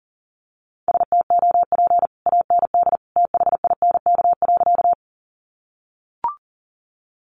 Morse code, heard as a keyed tone, about 40 wpm, 700 Hz. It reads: STOP AND THINK.
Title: Sentences from Top 300 Words 40wpm